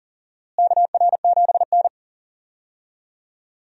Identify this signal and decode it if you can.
Morse code, heard as a keyed tone, about 38 wpm, 700 Hz. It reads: KR7N
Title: USA Call Signs - 4 Characters - Repeated 2x - 40wpm